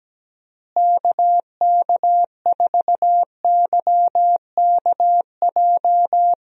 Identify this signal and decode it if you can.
Morse code, heard as a keyed tone, about 17 wpm, 700 Hz. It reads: KK4YKJ
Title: USA Call Signs - 6 Characters - Repeated 2x - 17wpm